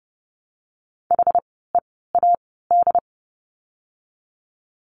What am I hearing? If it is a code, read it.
Morse code, heard as a keyed tone, about 28 wpm, spaced as for 17 wpm, 700 Hz. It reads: HEAD
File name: course / 30:10wpm / 115 Lesson 19f - Letter w - Words 30wpm